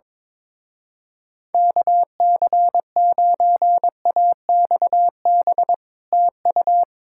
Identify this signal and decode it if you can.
Morse code, heard as a keyed tone, about 22 wpm, 700 Hz. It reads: KC9AXB TU
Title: Speed-Racing Common QSO Phrases - 22wpm